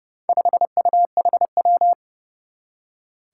Morse code, heard as a keyed tone, about 30 wpm, 700 Hz. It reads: 5UHW
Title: Four Challenging Letter-Number Combinations 30wpm